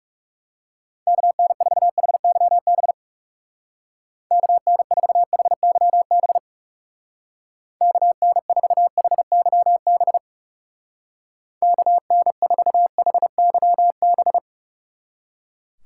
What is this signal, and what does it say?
Morse code, heard as a keyed tone, about 37 wpm, 700 Hz. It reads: KN4HYB KN4HYB KN4HYB KN4HYB
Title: Speed Racing - USA 2x3 Call Signs 30wpm